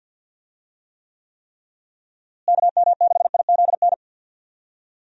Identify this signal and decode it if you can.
Morse code, heard as a keyed tone, about 52 wpm, 700 Hz. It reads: KM6IZN
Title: USA Call Signs - 6 Characters - Repeated 3x - 50wpm